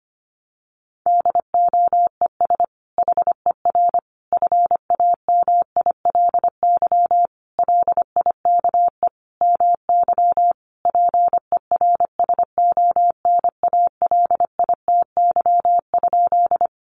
Morse code, heard as a keyed tone, about 25 wpm, 700 Hz. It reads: DOES HER FAMILY LIKE MY PERSONALITY?
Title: Sentences from Top 1400 Words 25wpm